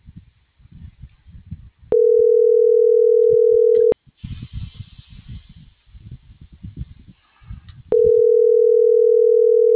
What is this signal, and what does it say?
Ringback tone (North America), 440 + 480 Hz.